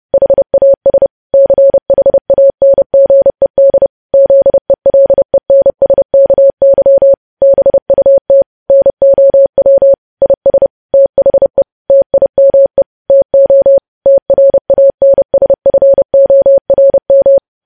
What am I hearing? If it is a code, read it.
Morse code, heard as a keyed tone, about 30 wpm, 550 Hz. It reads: HAS CHANGED ZELENSKY BUT NOW IS THE TIME TO TRANSFORM
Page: news headlines in morse code at 30 wpm